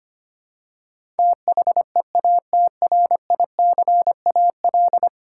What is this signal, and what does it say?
Morse code, heard as a keyed tone, about 25 wpm, 700 Hz. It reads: THEATRICAL
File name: Ten Letter Words - 25wpm